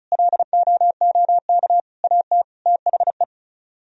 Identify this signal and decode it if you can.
Morse code, heard as a keyed tone, about 47 wpm, spaced as for 41 wpm, 700 Hz. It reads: LOOK AT THE
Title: Speed-Racing - Sets of 4 Words 35wpm